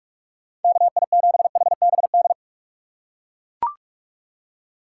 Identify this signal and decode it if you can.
Morse code, heard as a keyed tone, about 41 wpm, 700 Hz. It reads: KI7HBD
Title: USA Call Signs - 2x Character Spacing - 45wpm